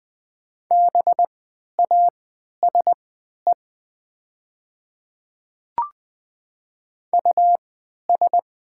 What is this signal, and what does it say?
Morse code, heard as a keyed tone, about 20 wpm, spaced as for 11 wpm, 700 Hz. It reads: BASE US
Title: 3x Character Spacing - Top 500 Words - 20wpm